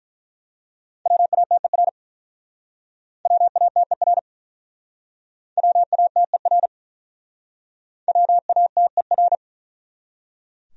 Morse code, heard as a keyed tone, about 42 wpm, 700 Hz. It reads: WATER WATER WATER WATER
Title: Speed-Racing Top 100 Words 35wpm